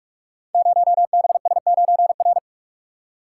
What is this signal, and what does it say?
Morse code, heard as a keyed tone, about 44 wpm, 690 Hz. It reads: 0BS9R
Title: Code Groups - 5 Letter:Numbers - 45wpm